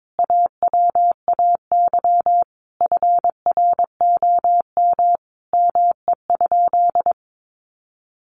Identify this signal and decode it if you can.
Morse code, heard as a keyed tone, about 22 wpm, 700 Hz. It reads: AWAY FROM ME?
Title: Sentences from Top 1500 Words 22wpm